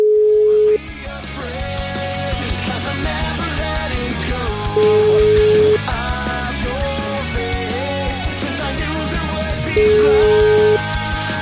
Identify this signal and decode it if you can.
Ringback tone (Europe, 425 Hz), with noise.